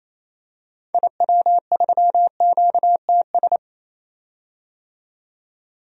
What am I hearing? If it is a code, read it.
Morse code, heard as a keyed tone, about 27 wpm, 700 Hz. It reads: IW3QTS
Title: DXCC-100 Call Signs - 6 Characters - 28wpm